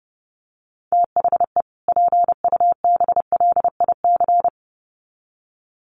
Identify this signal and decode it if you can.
Morse code, heard as a keyed tone, about 30 wpm, 700 Hz. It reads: THE PUBLIC
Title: Sets of 2 Words 30wpm